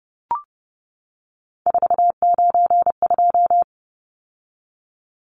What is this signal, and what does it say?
Morse code, heard as a keyed tone, about 30 wpm, 700 Hz. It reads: 492
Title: Numbers - 3-Digit 30wpm